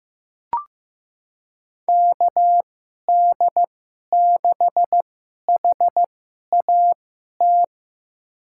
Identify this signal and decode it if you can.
Morse code, heard as a keyed tone, about 15 wpm, spaced as for 11 wpm, 700 Hz. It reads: KD6HAT
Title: USA Call Signs - 2x Character Spacing - 15wpm